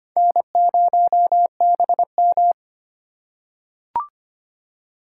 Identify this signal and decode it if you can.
Morse code, heard as a keyed tone, about 25 wpm, 700 Hz. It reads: N0BM
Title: USA Call Signs - 4 Characters - 2x Character Spacing - 25wpm